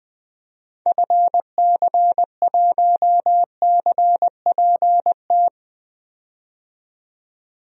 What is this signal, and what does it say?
Morse code, heard as a keyed tone, about 20 wpm, 700 Hz. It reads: FC1CPT
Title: DXCC-100 Call Signs - 6 Characters - 20wpm